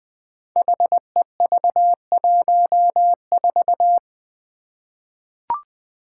Morse code, heard as a keyed tone, about 21 wpm, 700 Hz. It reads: HEV14
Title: Five Challenging Letter-Number Combinations 20wpm